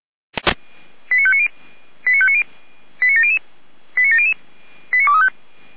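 Five-tone selective call (ZVEI), heard as quick runs of five tones, tones of about 70 ms. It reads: 89590 89590 89700 89700 89235
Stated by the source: ZVEI five-tone selective call 89020 received by SDR.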